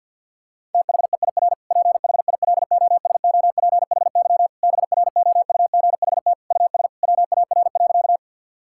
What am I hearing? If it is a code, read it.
Morse code, heard as a keyed tone, about 50 wpm, 700 Hz. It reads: THEIR PHILOSOPHY BROUGHT US WAR.
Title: Sentences from Top 1000 Words - Review Part 6 - 50wpm